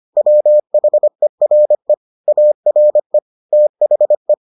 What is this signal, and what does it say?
Morse code, heard as a keyed tone, about 25 wpm, 600 Hz. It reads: WHERE ARE THE